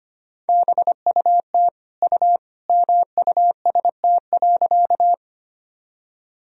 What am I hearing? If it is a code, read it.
Morse code, heard as a keyed tone, about 25 wpm, 700 Hz. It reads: BUT U MUST.
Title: Sentences from Top 200 Words and Abbreviations 25wpm